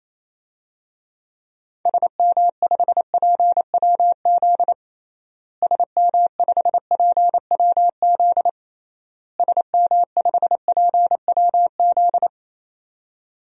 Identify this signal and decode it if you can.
Morse code, heard as a keyed tone, about 28 wpm, 700 Hz. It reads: SM5PWZ SM5PWZ SM5PWZ
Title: DXCC-100 Call Signs - 6 Characters - Repeated 3x - 28wpm